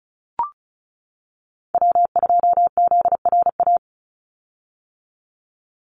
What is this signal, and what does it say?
Morse code, heard as a keyed tone, about 34 wpm, 700 Hz. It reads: W2ZRA
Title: Call Signs - 35wpm